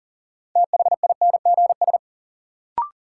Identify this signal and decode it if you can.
Morse code, heard as a keyed tone, about 38 wpm, 700 Hz. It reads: THINGS
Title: Sentences from Top 1000 Words - Review Part 4 - New Word Drill - 40wpm